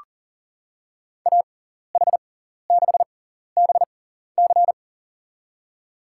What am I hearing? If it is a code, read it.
Morse code, heard as a keyed tone, about 40 wpm, spaced as for 14 wpm, 700 Hz. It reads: AH6BC
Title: USA Call Signs - 5 Characters - 6x Character Spacing - 40wpm